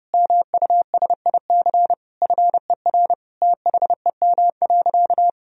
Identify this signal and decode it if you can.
Morse code, heard as a keyed tone, about 30 wpm, 700 Hz. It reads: MUSIC FER THEM.
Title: Sentences from Top 300 Words and Abbreviations 30wpm